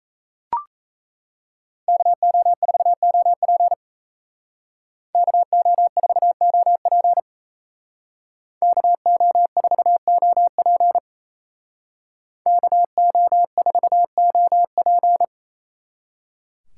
Morse code, heard as a keyed tone, about 36 wpm, 700 Hz. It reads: KO4OP KO4OP KO4OP KO4OP